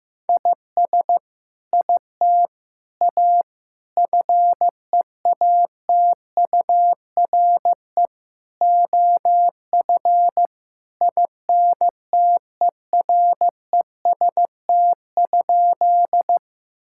Morse code, heard as a keyed tone, about 15 wpm, 700 Hz. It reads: IS IT A FEATURE OF INTEREST?
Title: Sentences from Top 1000 Words - Review Part 5 - 15wpm